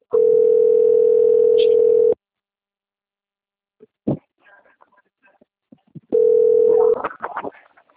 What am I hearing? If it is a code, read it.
Ringback tone (North America), 440 + 480 Hz.